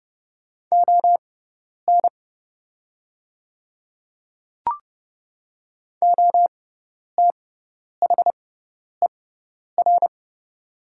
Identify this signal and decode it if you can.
Morse code, heard as a keyed tone, about 30 wpm, spaced as for 10 wpm, 700 Hz. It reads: ON OTHER
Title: 6x Character Spacing - Top 100 Words - 30wpm